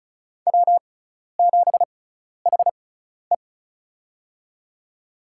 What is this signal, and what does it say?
Morse code, heard as a keyed tone, about 36 wpm, spaced as for 12 wpm, 700 Hz. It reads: W7HE
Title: USA Call Signs - 4 Characters - 6x Character Spacing - 35wpm